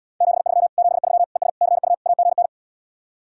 Morse code, heard as a keyed tone, about 77 wpm, 700 Hz. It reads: 73 73 ES CU AGN